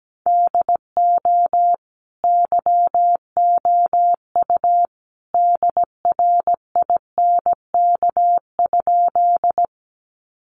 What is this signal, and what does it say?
Morse code, heard as a keyed tone, about 17 wpm, 700 Hz. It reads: DO YOU DRINK?
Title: Sentences from Top 1500 Words 17wpm